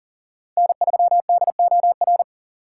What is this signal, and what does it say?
Morse code, heard as a keyed tone, about 40 wpm, 700 Hz. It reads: N3DOR
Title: USA Call Signs - 5 Characters - 2x Character Spacing - 40wpm